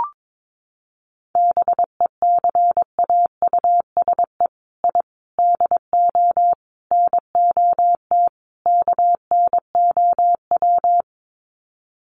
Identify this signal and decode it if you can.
Morse code, heard as a keyed tone, about 22 wpm, 700 Hz. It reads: BECAUSE I DO NOT KNOW
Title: Sets of 5 Words Encore2 22wpm